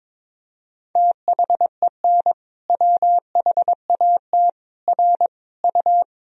Morse code, heard as a keyed tone, about 21 wpm, 700 Hz. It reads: THEN WHAT R U
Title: Sentences from Top 100 Words and Abbreviations 22wpm